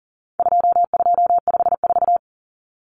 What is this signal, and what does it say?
Morse code, heard as a keyed tone, about 39 wpm, 700 Hz. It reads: 2254
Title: Common Cut Numbers - 4-Digit 40wpm